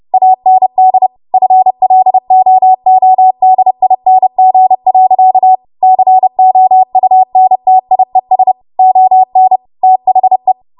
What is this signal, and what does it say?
Morse code, heard as a keyed tone, about 30 wpm, 750 Hz. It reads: AND FLOODING. COUNTIES ON THE
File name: Morse / 30WPM